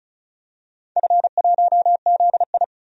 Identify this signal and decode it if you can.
Morse code, heard as a keyed tone, about 36 wpm, 690 Hz. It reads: F1ZI
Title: DXCC-100 Call Signs - 4 Characters - 35wpm